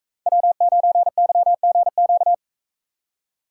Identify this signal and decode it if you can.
Morse code, heard as a keyed tone, about 45 wpm, 700 Hz. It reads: W9YGQ